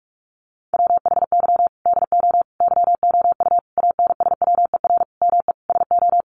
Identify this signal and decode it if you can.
Morse code, heard as a keyed tone, about 44 wpm, 700 Hz. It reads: WHY DO YOU ANSWER ME SO
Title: Sentences from Top 1500 Words 45wpm